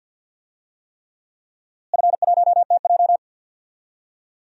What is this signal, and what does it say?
Morse code, heard as a keyed tone, about 52 wpm, 700 Hz. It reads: F1TJ
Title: DXCC-100 Call Signs - 4 Characters - Repeated 3x - 50wpm